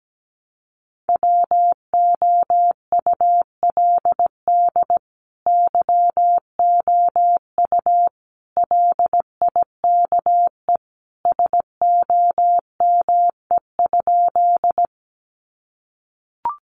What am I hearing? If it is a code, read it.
Morse code, heard as a keyed tone, about 17 wpm, 700 Hz. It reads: WOULD YOU LIKE SOME?
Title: Sentences from Top 100 Words 17wpm